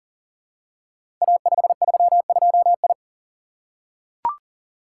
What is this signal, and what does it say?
Morse code, heard as a keyed tone, about 38 wpm, 690 Hz. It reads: A532I
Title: Code Groups - 5 Letter:Numbers - 40wpm